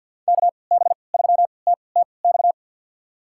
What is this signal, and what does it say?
Morse code, heard as a keyed tone, about 50 wpm, spaced as for 28 wpm, 700 Hz. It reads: KB3TTX
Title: USA Call Signs - 6 Characters - 3x Character Spacing - 50wpm